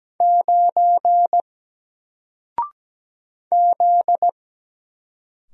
Morse code, heard as a keyed tone, about 17 wpm, 700 Hz. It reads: 9 Z
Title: Single Letter-Number Alt 17wpm